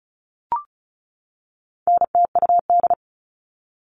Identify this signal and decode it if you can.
Morse code, heard as a keyed tone, about 34 wpm, 700 Hz. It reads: NTUD
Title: All Cut Numbers - 4-Digit 35wpm